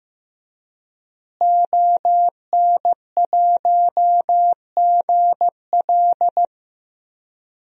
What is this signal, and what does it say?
Morse code, heard as a keyed tone, about 15 wpm, 700 Hz. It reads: ON1GL
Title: DXCC-100 Call Signs - 5 Characters - Repeated 2x - 15wpm